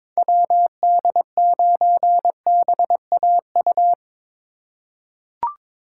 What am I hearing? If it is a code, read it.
Morse code, heard as a keyed tone, about 22 wpm, 700 Hz. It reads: WD9BAU
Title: USA Call Signs - 6 Characters - 3x Character Spacing - 22wpm